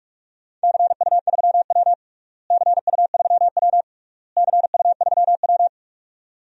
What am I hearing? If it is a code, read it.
Morse code, heard as a keyed tone, about 48 wpm, 700 Hz. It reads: CU3W CU3W CU3W
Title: DXCC-100 Call Signs - 4 Characters - Repeated 3x - 45wpm